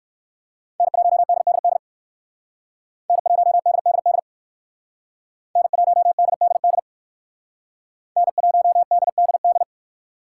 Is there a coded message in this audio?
Yes: Morse code from 0.8 to 9.6 s, heard as a keyed tone, about 60 wpm, 700 Hz. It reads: N1DDD N1DDD N1DDD N1DDD